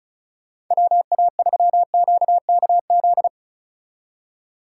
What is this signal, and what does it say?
Morse code, heard as a keyed tone, about 46 wpm, 700 Hz. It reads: WA3QKZ